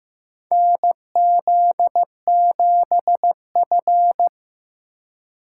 Morse code, heard as a keyed tone, about 16 wpm, 700 Hz. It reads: NZ7F
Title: USA Call Signs - 3x Character Spacing - 15wpm